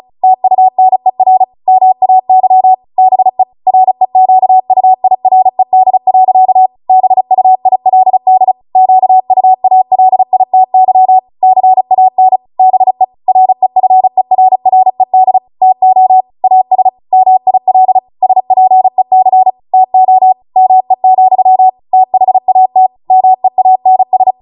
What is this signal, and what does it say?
Morse code, heard as a keyed tone, about 36 wpm, 750 Hz. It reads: TUNER MAY BE REQUIRED. BUILD QUALITY CAN BE REFERRED TO AS MIL SPEC TO ME, THAT MEANS